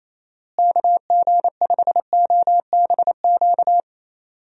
Morse code, heard as a keyed tone, about 28 wpm, 700 Hz. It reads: KG5OBQ
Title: USA Call Signs - 6 Characters - Repeated 2x - 28wpm